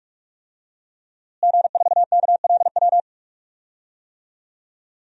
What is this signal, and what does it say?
Morse code, heard as a keyed tone, about 46 wpm, 700 Hz. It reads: G4KLW
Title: DXCC-100 Call Signs - 5 Characters - Repeated 3x - 45wpm